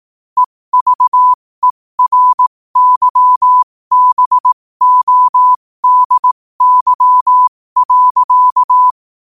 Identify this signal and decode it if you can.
Morse code, heard as a keyed tone, about 18 wpm, spaced as for 15 wpm, 1000 Hz. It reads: EVERYBODY.
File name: qotdfiles / 15wpm